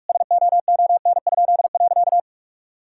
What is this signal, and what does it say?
Morse code, heard as a keyed tone, about 48 wpm, 700 Hz. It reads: SOON?.
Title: Pangram Challenge v2 - 45wpm